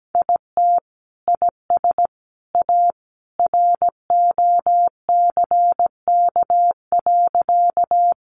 Morse code, heard as a keyed tone, about 17 wpm, 700 Hz. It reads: IT IS A ROCK.
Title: Sentences from Top 400 Words 17wpm